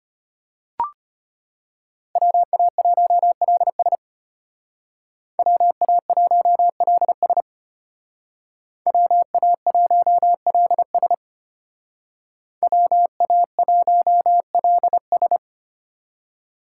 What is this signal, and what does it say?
Morse code, heard as a keyed tone, about 31 wpm, 700 Hz. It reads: WA1LS WA1LS WA1LS WA1LS
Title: Speed Racing - USA 2x2 Call Signs 25wpm